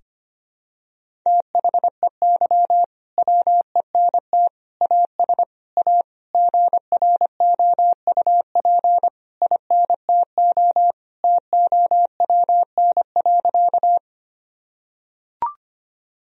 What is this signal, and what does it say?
Morse code, heard as a keyed tone, about 25 wpm, 700 Hz. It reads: THEY WENT AS A GROUP INTO TOWN.
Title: Sentences from Top 1000 Words - Review Part 1 - 25wpm